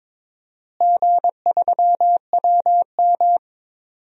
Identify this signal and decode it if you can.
Morse code, heard as a keyed tone, about 22 wpm, 700 Hz. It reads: G3WM
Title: DXCC-100 Call Signs - 4 Characters - Repeated 2x - 22wpm